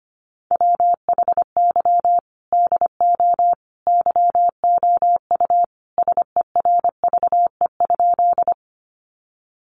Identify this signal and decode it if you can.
Morse code, heard as a keyed tone, about 25 wpm, 700 Hz. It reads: WHY DO YOU SERVE?
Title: Sentences from Top 400 Words 25wpm